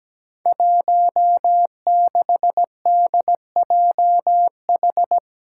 Morse code, heard as a keyed tone, about 17 wpm, 700 Hz. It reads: 16DJH
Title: Five Challenging Letter-Number Combinations 17wpm